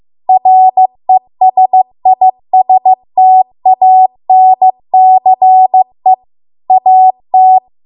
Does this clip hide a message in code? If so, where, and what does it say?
Yes: Morse code from 0.3 to 7.6 s, heard as a keyed tone, about 15 wpm, 750 Hz. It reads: RESISTANCE AT